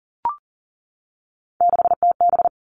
Morse code, heard as a keyed tone, about 38 wpm, 700 Hz. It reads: 6TB
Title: Three Challenging Letter-Number Combinations 40wpm